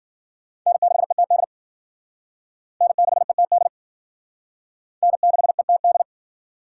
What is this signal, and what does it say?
Morse code, heard as a keyed tone, about 50 wpm, 700 Hz. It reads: N6ETD N6ETD N6ETD